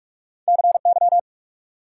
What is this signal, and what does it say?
Morse code, heard as a keyed tone, about 63 wpm, spaced as for 54 wpm, 690 Hz. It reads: CY
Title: Speed-Racing Multi-letter Phonemes - 45wpm